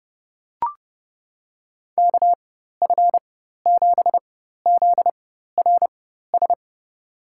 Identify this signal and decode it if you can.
Morse code, heard as a keyed tone, about 30 wpm, spaced as for 14 wpm, 700 Hz. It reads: KF7ZRS
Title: USA Call Signs - 6 Characters - 4x Character Spacing - 30wpm